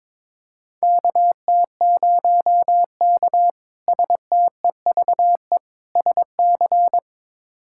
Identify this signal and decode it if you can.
Morse code, heard as a keyed tone, about 22 wpm, 700 Hz. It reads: KT0K STEVE SC
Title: K1USN SST QSO Practice - 22wpm